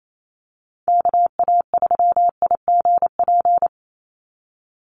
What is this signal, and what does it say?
Morse code, heard as a keyed tone, about 28 wpm, 700 Hz. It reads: KA3IGP
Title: USA Call Signs 28wpm